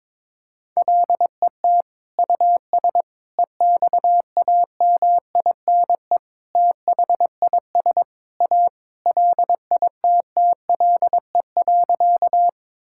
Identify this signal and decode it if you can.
Morse code, heard as a keyed tone, about 22 wpm, 700 Hz. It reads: LET US EXAMINE THIS A LITTLE.
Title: Sentences from Top 1600 Words 22wpm